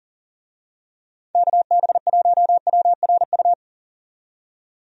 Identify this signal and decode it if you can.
Morse code, heard as a keyed tone, about 38 wpm, 700 Hz. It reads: KB1WRU
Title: USA Call Signs - 6 Characters - 40wpm